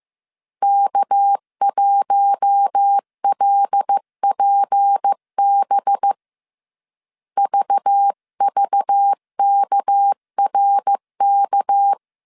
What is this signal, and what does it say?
Morse code, heard as a keyed tone, about 15 wpm, 800 Hz. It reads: K1LPB VVKRK